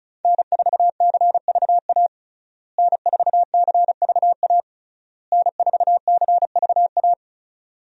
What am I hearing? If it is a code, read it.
Morse code, heard as a keyed tone, about 36 wpm, 700 Hz. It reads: N4CVA N4CVA N4CVA
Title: USA Call Signs - 5 Characters - Repeated 3x - 35wpm